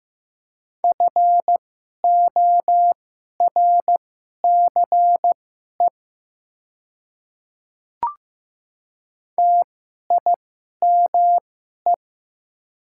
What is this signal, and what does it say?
Morse code, heard as a keyed tone, about 15 wpm, spaced as for 11 wpm, 700 Hz. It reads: FORCE TIME
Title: 2x Character Spacing - Top 500 Words - 15wpm